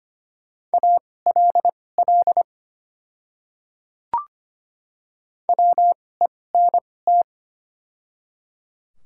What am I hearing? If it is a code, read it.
Morse code, heard as a keyed tone, about 25 wpm, spaced as for 18 wpm, 690 Hz. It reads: ALL WENT
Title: 2x Character Spacing - Top 300 Words - 25wpm